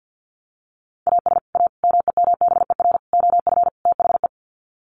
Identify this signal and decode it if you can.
Morse code, heard as a keyed tone, about 50 wpm, 700 Hz. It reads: AS A MEMBER OF THE
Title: Sets of 5 Words 50wpm